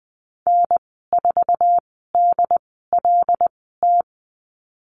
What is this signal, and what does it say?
Morse code, heard as a keyed tone, about 20 wpm, spaced as for 15 wpm, 700 Hz. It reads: N4DLT
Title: 088 Lesson 15g - Number 5 - Call Signs 20wpm